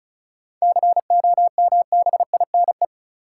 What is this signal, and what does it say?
Morse code, heard as a keyed tone, about 34 wpm, 700 Hz. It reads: COMBINE
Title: Seven Letter Words - 35wpm